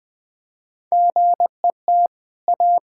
Morse code, heard as a keyed tone, about 21 wpm, 700 Hz. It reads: GET A
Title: Easy Sentences with Hints - 20wpm